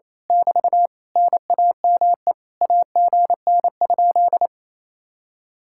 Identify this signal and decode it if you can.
Morse code, heard as a keyed tone, about 27 wpm, 700 Hz. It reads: <BT> NAME AGN?
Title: Speed-Racing Common QSO Phrases - 28wpm